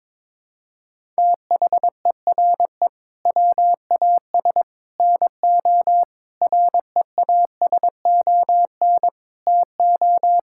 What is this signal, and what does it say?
Morse code, heard as a keyed tone, about 22 wpm, 700 Hz. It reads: THERE WAS NO REASON TO
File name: practice / 22wpm / Sentences with 3-Word Phrases - 22wpm